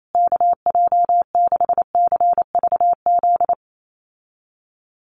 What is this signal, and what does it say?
Morse code, heard as a keyed tone, about 28 wpm, 700 Hz. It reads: KJ6CVZ
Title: USA Call Signs Repeated 2x 28wpm